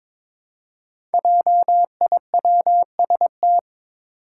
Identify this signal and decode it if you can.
Morse code, heard as a keyed tone, about 22 wpm, 700 Hz. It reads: JIWST
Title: Five Challenging Letter-Number Combinations 22wpm